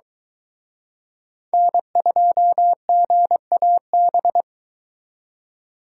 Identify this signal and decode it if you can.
Morse code, heard as a keyed tone, about 23 wpm, 700 Hz. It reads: N2GAB